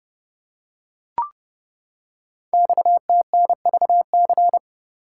Morse code, heard as a keyed tone, about 30 wpm, 700 Hz. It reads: XTNVC
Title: Code Groups - 5 Letter:Numbers - 30wpm